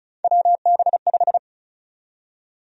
Morse code, heard as a keyed tone, about 36 wpm, 700 Hz. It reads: WB5
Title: USA Call Sign Prefixes - 35wpm